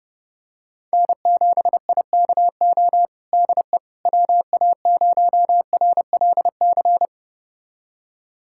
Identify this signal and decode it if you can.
Morse code, heard as a keyed tone, about 29 wpm, 700 Hz. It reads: N7IKO DE WA0RLC
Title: USA Callsign DE Callsign 30wpm